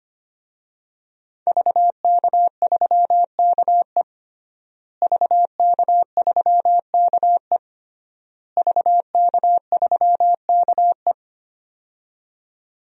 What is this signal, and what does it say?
Morse code, heard as a keyed tone, about 25 wpm, 700 Hz. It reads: VK3KE VK3KE VK3KE
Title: DXCC-100 Call Signs - 5 Characters - Repeated 3x - 25wpm